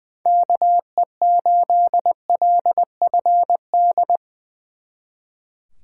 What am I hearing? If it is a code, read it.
Morse code, heard as a keyed tone, about 23 wpm, 700 Hz. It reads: KE8LFD
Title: Speed Racing - USA Call Signs 20wpm